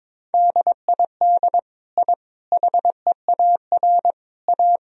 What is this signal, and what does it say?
Morse code, heard as a keyed tone, about 22 wpm, 700 Hz. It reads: DID I HEAR A
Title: Sentences from Top 1000 Words - Review Part 2 - 22wpm